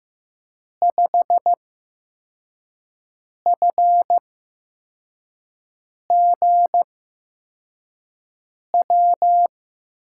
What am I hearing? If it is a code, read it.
Morse code, heard as a keyed tone, about 15 wpm, spaced as for 4 wpm, 700 Hz. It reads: 5FGW
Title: USA Call Signs - 8x Character Spacing - 15wpm